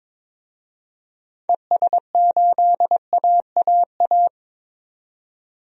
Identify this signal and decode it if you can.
Morse code, heard as a keyed tone, about 22 wpm, 700 Hz. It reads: ES8AAA
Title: DXCC-100 Call Signs - 6 Characters - 22wpm